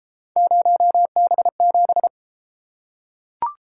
Morse code, heard as a keyed tone, about 32 wpm, 700 Hz. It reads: 0B7